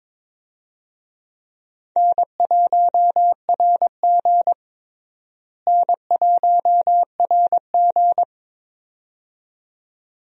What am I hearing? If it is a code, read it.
Morse code, heard as a keyed tone, about 22 wpm, 700 Hz. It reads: N1RG N1RG
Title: USA Call Signs - 4 Characters - Repeated 2x - 22wpm